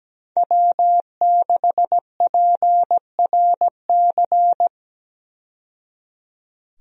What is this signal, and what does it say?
Morse code, heard as a keyed tone, about 17 wpm, 700 Hz. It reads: W6PRC
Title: USA Call Signs - 5 Characters - 17wpm